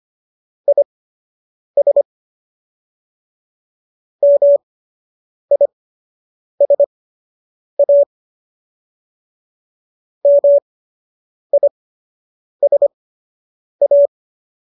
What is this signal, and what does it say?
Morse code, heard as a keyed tone, about 25 wpm, spaced as for 8 wpm, 570 Hz. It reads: IS MISA MISA